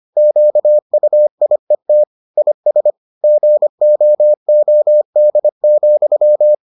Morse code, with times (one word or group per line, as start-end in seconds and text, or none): 0.2-2.0 QUIET
2.4-2.9 IS
3.2-6.5 GOOD,